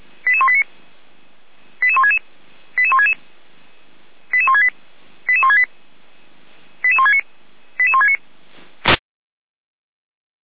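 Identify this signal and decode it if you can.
Five-tone selective call (ZVEI), heard as quick runs of five tones, tones of about 70 ms. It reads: 88189 88166 88166 88168 88168 88169 88169
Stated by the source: ZVEI five-tone call 88564 received on SDR.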